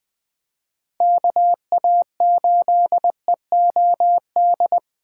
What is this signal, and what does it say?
Morse code, heard as a keyed tone, about 20 wpm, 700 Hz. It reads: KA8EOD
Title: Speed Racing - USA Call Signs 17wpm